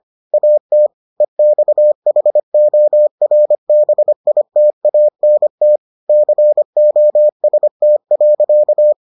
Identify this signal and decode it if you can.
Morse code, heard as a keyed tone, about 25 wpm, 600 Hz. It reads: AT EXHORBITANT COST.